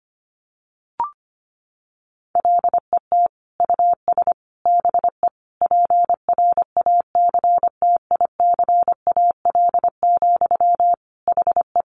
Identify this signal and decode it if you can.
Morse code, heard as a keyed tone, about 25 wpm, 700 Hz. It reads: LET US BE PRACTICAL, HE
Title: Sentences from Top 1400 Words 25wpm